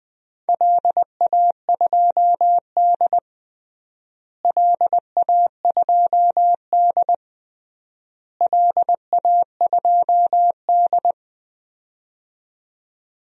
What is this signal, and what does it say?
Morse code, heard as a keyed tone, about 20 wpm, 700 Hz. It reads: LA2D LA2D LA2D
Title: DXCC-100 Call Signs - 4 Characters - Repeated 3x - 20wpm